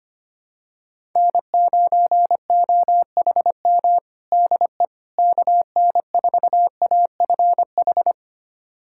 Morse code, heard as a keyed tone, about 25 wpm, 700 Hz. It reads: N9OHM DE KN4AFH
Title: USA Callsign DE Callsign 25wpm